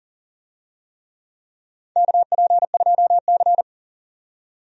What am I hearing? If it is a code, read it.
Morse code, heard as a keyed tone, about 40 wpm, 700 Hz. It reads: KP2C
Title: DXCC-100 Call Signs - 4 Characters - 40wpm